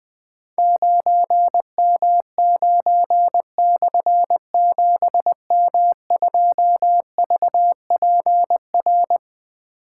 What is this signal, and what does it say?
Morse code, heard as a keyed tone, about 20 wpm, 700 Hz. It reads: 9M9/7M2VPR
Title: Difficult Call Signs 20wpm